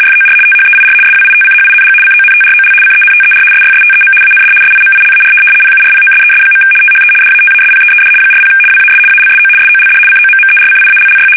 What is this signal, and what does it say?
RTTY, about 75 baud. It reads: FOR THEIR SWITCHING CONTROL CENTER SYSTEM AND SIMILAR PURPOSES. THE DATASPEED 40 WAS ALSO SOLD COMMERCIALLY
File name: fsk850-75B / teletype-corp